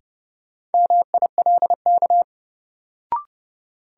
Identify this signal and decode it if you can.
Morse code, heard as a keyed tone, about 32 wpm, 700 Hz. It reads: MILK
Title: Sentences from Top 1500 Words - New Word Drill - 30wpm